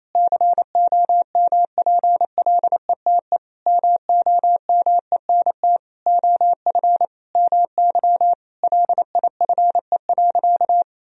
Morse code, heard as a keyed tone, about 28 wpm, 700 Hz. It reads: COMPLETE MOMENT OF MY LIFE.
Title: Sentences from Top 1000 Words - Review Part 2 - 28wpm